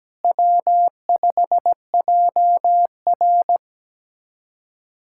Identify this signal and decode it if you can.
Morse code, heard as a keyed tone, about 17 wpm, 700 Hz. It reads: W5JR
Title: USA Callsign DE Callsign 17wpm